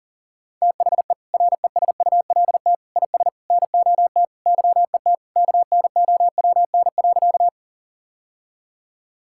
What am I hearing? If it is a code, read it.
Morse code, heard as a keyed tone, about 37 wpm, 700 Hz. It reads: THE RESULT IS NOT YET KNOWN.
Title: Sentences from Top 1000 Words - Review Part 5 - 40wpm